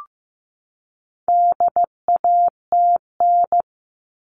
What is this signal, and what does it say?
Morse code, heard as a keyed tone, about 15 wpm, 700 Hz. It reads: DATN
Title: All Cut Numbers - 4-Digit 15wpm